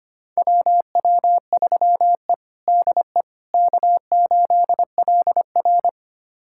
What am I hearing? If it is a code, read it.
Morse code, heard as a keyed tone, about 25 wpm, 700 Hz. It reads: WW3E DE K8LR
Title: USA Callsign DE Callsign - 4 Characters - 25wpm